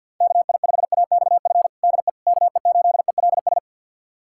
Morse code, heard as a keyed tone, about 51 wpm, 700 Hz. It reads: KI5AXF DE KE8ELS
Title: USA Callsign DE Callsign - 6 Characters - 50wpm